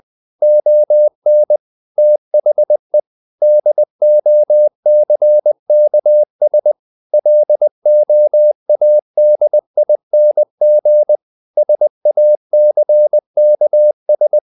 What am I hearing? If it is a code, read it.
Morse code, heard as a keyed tone, about 20 wpm, 600 Hz. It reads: ON THE DOCKS LOADING SACKS
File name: the old man and the sea part 2 20wpm